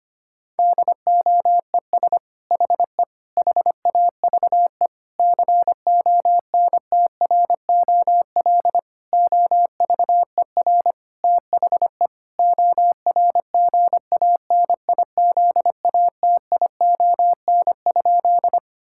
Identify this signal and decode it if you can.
Morse code, heard as a keyed tone, about 25 wpm, 700 Hz. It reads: DOES HE HAVE CONTROL OVER THE ORGANIZATION?
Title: Sentences from Top 1000 Words - Review Part 5 - 25wpm